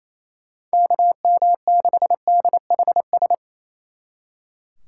Morse code, heard as a keyed tone, about 37 wpm, 700 Hz. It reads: KM6DHS